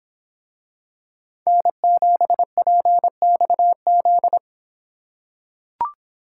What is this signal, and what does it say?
Morse code, heard as a keyed tone, about 25 wpm, 700 Hz. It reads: N7PXZ